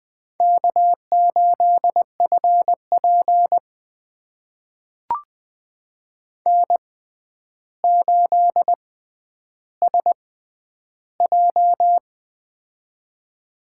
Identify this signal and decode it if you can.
Morse code, heard as a keyed tone, about 20 wpm, spaced as for 10 wpm, 700 Hz. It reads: K8FP N8SJ
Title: USA Call Signs - 4 Characters - 6x Character Spacing - 20wpm